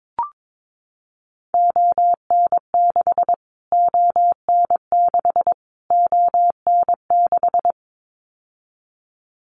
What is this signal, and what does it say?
Morse code, heard as a keyed tone, about 22 wpm, 700 Hz. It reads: ON6 ON6 ON6
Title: Callsign Prefixes for Top 25 DXCC - Repeated 3x - 22wpm